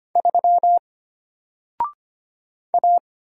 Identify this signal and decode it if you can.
Morse code, heard as a keyed tone, about 25 wpm, 700 Hz. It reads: 3 A
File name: Single-Letter-Number-25wpm